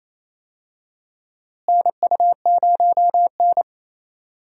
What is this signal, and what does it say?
Morse code, heard as a keyed tone, about 28 wpm, 700 Hz. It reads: NU0N
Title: USA Call Signs - 4 Characters - Repeated 2x - 28wpm